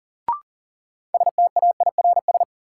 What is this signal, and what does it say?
Morse code, heard as a keyed tone, about 40 wpm, 700 Hz. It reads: STAIRS
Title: Sets of 5 Words Encore - New Word Drill - 40wpm